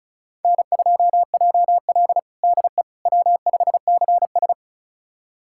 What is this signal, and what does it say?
Morse code, heard as a keyed tone, about 36 wpm, 700 Hz. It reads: N2JL DE W5CS
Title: USA Callsign DE Callsign - 4 Characters - 35wpm